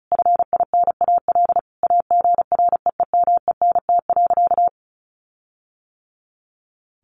Morse code, heard as a keyed tone, about 34 wpm, 700 Hz. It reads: FINAL AGREEMENT.
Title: Sentences from Top 1500 Words 35wpm